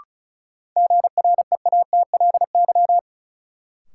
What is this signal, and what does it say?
Morse code, heard as a keyed tone, about 34 wpm, 700 Hz. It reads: GREATLY
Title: Sentences from Top 1400 Words - New Word Drill - 35wpm